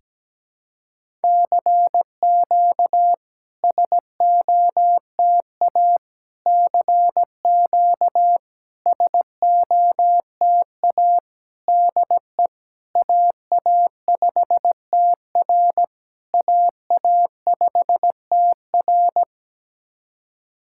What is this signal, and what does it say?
Morse code, heard as a keyed tone, about 17 wpm, 690 Hz. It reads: CQ SOTA CQ SOTA DE AA5TR AA5TR
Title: SOTA QSO Practice 17wpm